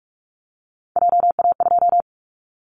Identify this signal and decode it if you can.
Morse code, heard as a keyed tone, about 46 wpm, 700 Hz. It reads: JA2
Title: Callsign Prefixes for Top 50 DXCC - 45wpm